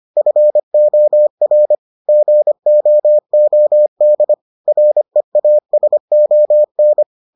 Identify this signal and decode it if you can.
Morse code, heard as a keyed tone, about 25 wpm, 600 Hz. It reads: FOR GOOD REASON